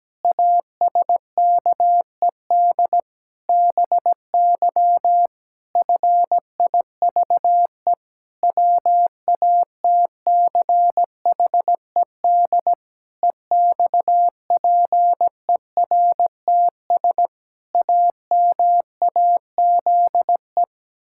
Morse code, heard as a keyed tone, about 17 wpm, 700 Hz. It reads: ASKED BY FIVE WATCHED EXPERTS AMAZE
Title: Pangram Challenge v2 - 17wpm